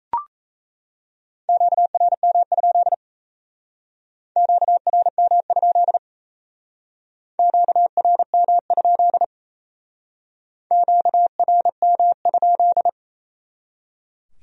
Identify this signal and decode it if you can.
Morse code, heard as a keyed tone, about 35 wpm, 700 Hz. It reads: QRM? QRM? QRM? QRM?